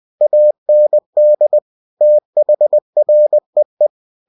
Morse code, heard as a keyed tone, about 20 wpm, 600 Hz. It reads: AND THREE